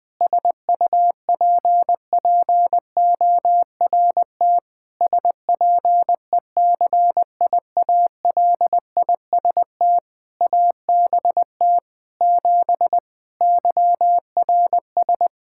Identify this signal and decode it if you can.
Morse code, heard as a keyed tone, about 20 wpm, 700 Hz. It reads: SUPPORT SPECIALIST ABT 7 YRS
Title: Practice QSO - Exchange 3 - More Info - 20wpm